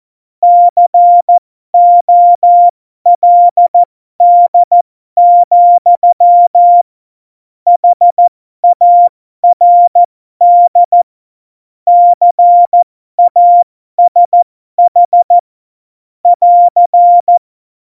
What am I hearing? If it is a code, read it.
Morse code, heard as a keyed tone, about 14 wpm, spaced as for 12 wpm, 700 Hz. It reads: COLD, HARD CASH <AR>
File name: Morsekurs / Witze